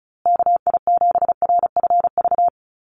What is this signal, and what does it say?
Morse code, heard as a keyed tone, about 34 wpm, 700 Hz. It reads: KI7RFV
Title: USA Call Signs Repeated 2x 35wpm